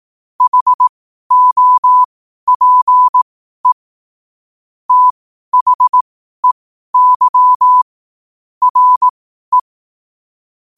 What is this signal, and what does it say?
Morse code, heard as a keyed tone, about 18 wpm, spaced as for 12 wpm, 1000 Hz. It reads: HOPE THEY RE